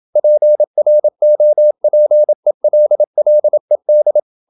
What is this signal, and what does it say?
Morse code, heard as a keyed tone, about 27 wpm, 600 Hz. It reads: PROPELLED
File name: Never 0027wpm